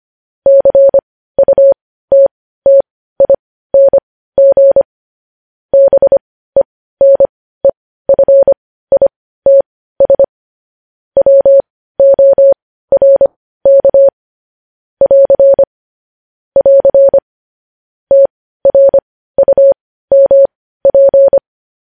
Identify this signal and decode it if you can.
Morse code, heard as a keyed tone, about 25 wpm, spaced as for 15 wpm, 550 Hz. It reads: CUTTING BENEFITS WORK <AR> <AR> TRUMP
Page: news headlines in morse code at 15 wpm